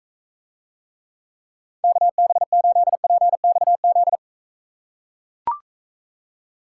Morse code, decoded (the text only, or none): KB8PXZ